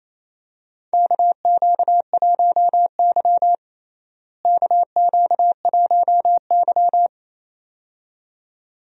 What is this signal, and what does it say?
Morse code, heard as a keyed tone, about 28 wpm, 700 Hz. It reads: KQ1Y KQ1Y
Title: USA Call Signs - 4 Characters - Repeated 2x - 28wpm